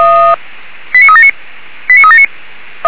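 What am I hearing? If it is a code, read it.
Five-tone selective call (ZVEI), heard as quick runs of five tones, tones of about 70 ms. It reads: 89379 89379
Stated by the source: ZVEI five-tone selective call received by SDR, fire brigade Orlamünde (feuerwehr-orlamuende).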